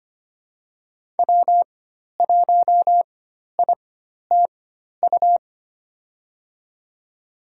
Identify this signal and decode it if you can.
Morse code, heard as a keyed tone, about 25 wpm, spaced as for 12 wpm, 700 Hz. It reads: W1ITU
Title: USA Call Signs - 5 Characters - 4x Character Spacing - 25wpm